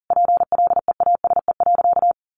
Morse code, heard as a keyed tone, about 40 wpm, 700 Hz. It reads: PLEASE.
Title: Sentences from Top 1500 Words 40wpm